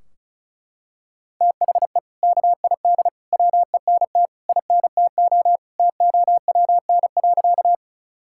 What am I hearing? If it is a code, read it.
Morse code, heard as a keyed tone, about 34 wpm, 700 Hz. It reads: THE KID WENT INTO TOWN.
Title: Sentences from Top 1600 Words 35wpm